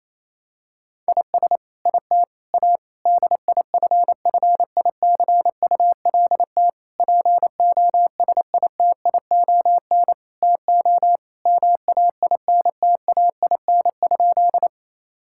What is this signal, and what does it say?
Morse code, heard as a keyed tone, about 28 wpm, 700 Hz. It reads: IS IT A DIFFICULT POSITION TO MAINTAIN?
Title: Sentences from Top 1000 Words - Review Part 5 - 28wpm